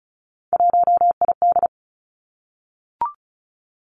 Morse code, heard as a keyed tone, about 36 wpm, 700 Hz. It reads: 1ID
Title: Three Challenging Letter-Number Combinations 35wpm